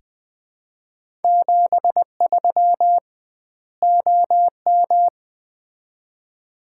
Morse code, heard as a keyed tone, about 20 wpm, 700 Hz. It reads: 73 OM
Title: Common QSO Phrases - 2x Word-Spacing - 20wpm